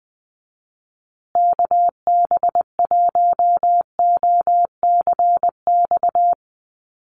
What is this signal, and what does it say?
Morse code, heard as a keyed tone, about 20 wpm, 700 Hz. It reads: KB1OCX
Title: USA Call Signs Repeated 3x 20wpm